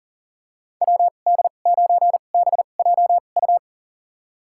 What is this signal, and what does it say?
Morse code, heard as a keyed tone, about 40 wpm, spaced as for 29 wpm, 700 Hz. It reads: WD9BJU
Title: USA Call Signs - 6 Characters - 2x Character Spacing - 40wpm